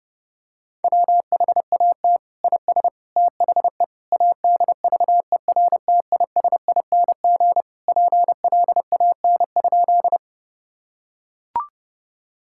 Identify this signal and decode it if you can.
Morse code, heard as a keyed tone, about 30 wpm, 700 Hz. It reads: WHAT IS THE ADVERTISING PLAN?
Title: Sentences from Top 1000 Words - Review Part 6 - 30wpm